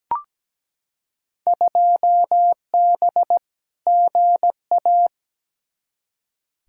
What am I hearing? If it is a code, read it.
Morse code, heard as a keyed tone, about 17 wpm, 700 Hz. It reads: 2B GA
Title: Field Day Exchanges Encore - 17wpm